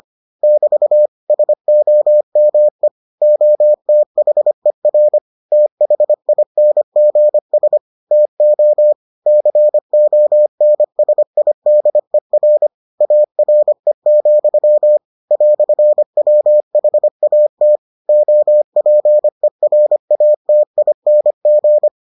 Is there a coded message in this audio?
Yes: Morse code from 0.4 to 21.9 s, heard as a keyed tone, about 25 wpm, 600 Hz. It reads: <BT> SOME OTHER THINGS TO CONSIDER ARE, "WHAT OPERATING